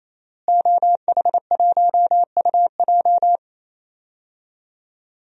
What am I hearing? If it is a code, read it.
Morse code, heard as a keyed tone, about 28 wpm, 700 Hz. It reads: OH1UJ
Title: DXCC-100 Call Signs - 5 Characters - Repeated 3x - 28wpm